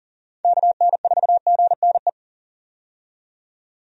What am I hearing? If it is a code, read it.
Morse code, heard as a keyed tone, about 38 wpm, 700 Hz. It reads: KN4GNE
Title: USA Call Signs - 6 Characters - 40wpm